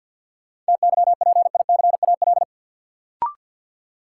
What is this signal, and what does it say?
Morse code, heard as a keyed tone, about 48 wpm, 700 Hz. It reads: TYPICAL
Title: Sentences from Top 1100 Words - New Word Drill Alt - 50wpm